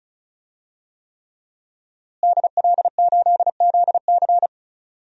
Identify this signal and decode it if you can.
Morse code, heard as a keyed tone, about 36 wpm, 700 Hz. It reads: DL8ZC
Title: DXCC-100 Call Signs - 5 Characters - Repeated 3x - 35wpm